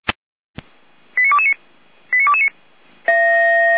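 Five-tone selective call (ZVEI), heard as quick runs of five tones, tones of about 70 ms. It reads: 89229 89229
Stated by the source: ZVEI five-tone selective call received by SDR, fire brigade Feuerwehr Orlamünde.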